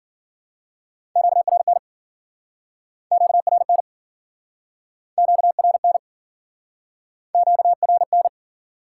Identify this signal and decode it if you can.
Morse code, heard as a keyed tone, about 52 wpm, 700 Hz. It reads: QRN QRN QRN QRN